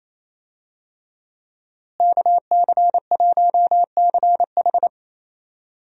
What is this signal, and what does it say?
Morse code, heard as a keyed tone, about 28 wpm, 700 Hz. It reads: KC1CH
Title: USA Call Signs - 5 Characters - Repeated 3x - 28wpm